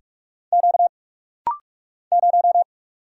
Morse code, heard as a keyed tone, about 44 wpm, 700 Hz. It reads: Q 0
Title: Single Letter-Number Alt - Rapid-Fire 45wpm